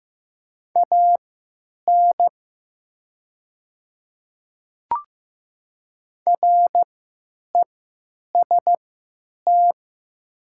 Morse code, heard as a keyed tone, about 15 wpm, spaced as for 9 wpm, 700 Hz. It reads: AN REST
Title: 3x Character Spacing - Top 500 Words - 15wpm